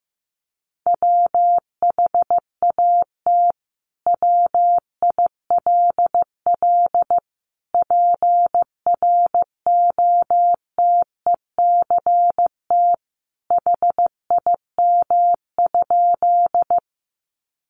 Morse code, heard as a keyed tone, about 15 wpm, 700 Hz. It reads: WHAT WILL PROTECT HIM?
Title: Sentences from Top 900 Words 15wpm